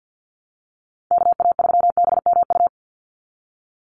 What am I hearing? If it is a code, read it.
Morse code, heard as a keyed tone, about 50 wpm, 700 Hz. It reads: KA3BMU
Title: Call Signs Repeated 2x - 50wpm